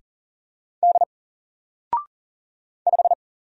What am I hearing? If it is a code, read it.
Morse code, heard as a keyed tone, about 40 wpm, 700 Hz. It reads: D 5
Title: Single Letter-Number 40wpm